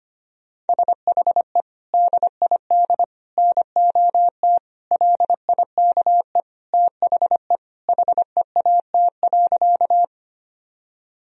Morse code, heard as a keyed tone, about 25 wpm, 700 Hz. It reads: SHE DID NOT LIKE THE HEAT.
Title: Sentences from Top 1000 Words - Review Part 3 - 25wpm